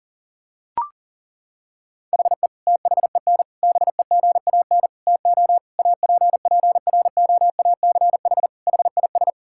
Morse code, heard as a keyed tone, about 37 wpm, 700 Hz. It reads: HE THEN BEGAN TO APPROACH HIS
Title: Sentences from Top 1000 Words - Review Part 6 - 40wpm